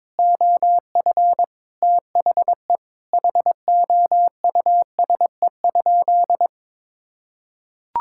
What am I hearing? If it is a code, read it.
Morse code, heard as a keyed tone, about 22 wpm, 700 Hz. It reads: OF THE HOUSE?
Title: Sentences from Top 1600 Words 22wpm